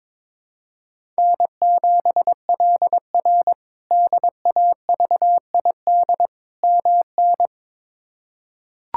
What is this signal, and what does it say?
Morse code, heard as a keyed tone, about 22 wpm, 700 Hz. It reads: N7LR DAVID MN
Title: K1USN SST QSO Practice - 22wpm